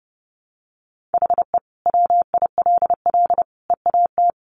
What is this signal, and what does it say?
Morse code, heard as a keyed tone, about 30 wpm, 700 Hz. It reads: HE WILL EAT
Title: Sentences from Top 1000 Words 30wpm